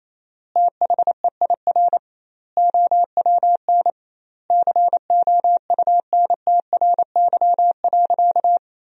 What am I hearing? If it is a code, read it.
Morse code, heard as a keyed tone, about 28 wpm, 700 Hz. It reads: THEIR OWN COUNTRY.
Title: 2x Word-Spacing - Sentences from Top 300 Words - 28wpm